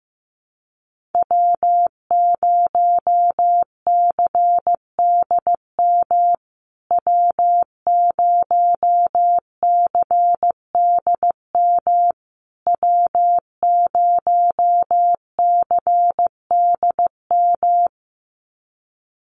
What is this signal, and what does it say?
Morse code, heard as a keyed tone, about 15 wpm, 700 Hz. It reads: W0CDM W0CDM W0CDM
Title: USA Call Signs Repeated 3x 15wpm